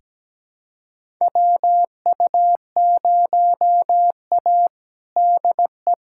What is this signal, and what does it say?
Morse code, heard as a keyed tone, about 17 wpm, 700 Hz. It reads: WU0A DE
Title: USA Callsign DE Callsign - 4 Characters - 17wpm